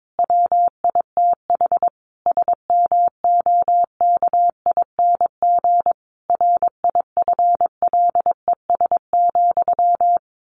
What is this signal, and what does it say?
Morse code, heard as a keyed tone, about 22 wpm, 700 Hz. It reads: WITH SMOKING RIFLES,
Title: A Sound Of Thunder 22wpm